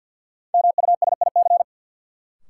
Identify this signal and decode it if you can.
Morse code, heard as a keyed tone, about 53 wpm, 700 Hz. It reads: MUSIC
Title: Sentences from Top 300 Words - New Word Drill - 50wpm